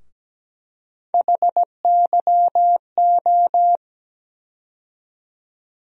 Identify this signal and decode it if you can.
Morse code, heard as a keyed tone, about 17 wpm, 700 Hz. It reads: HYO
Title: USA Call Sign Postfixes - 3 Letters 17wpm